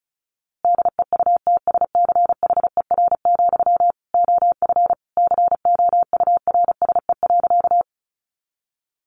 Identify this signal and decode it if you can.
Morse code, heard as a keyed tone, about 34 wpm, 700 Hz. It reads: DEUTSCHER, OF COURSE.
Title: A Sound Of Thunder 35wpm